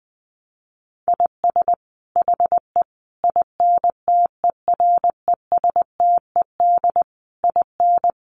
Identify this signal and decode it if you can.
Morse code, heard as a keyed tone, about 20 wpm, 700 Hz. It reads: IS HE INTERESTED IN
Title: Sentences from Top 900 Words 20wpm